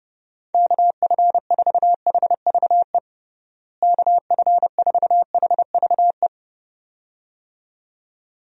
Morse code, heard as a keyed tone, about 30 wpm, 700 Hz. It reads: KF4HVE KF4HVE
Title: USA Call Signs - 6 Characters - Repeated 2x - 30wpm